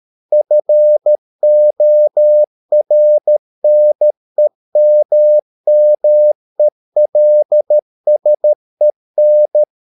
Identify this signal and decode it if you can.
Morse code, heard as a keyed tone, about 13 wpm, 600 Hz. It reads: FORNEMMELSEN
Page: Ferie 13 wpm